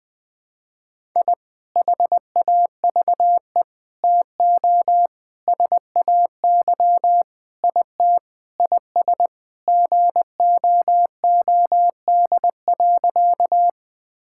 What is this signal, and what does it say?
Morse code, heard as a keyed tone, about 20 wpm, 700 Hz. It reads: I HAVE TO SAY IT IS GOOD.
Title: Sentences with 4-Word Phrases - 20wpm